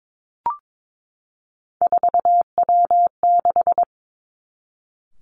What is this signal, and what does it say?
Morse code, heard as a keyed tone, about 22 wpm, 700 Hz. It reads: 4W6
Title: Three Challenging Letter-Number Combinations 22wpm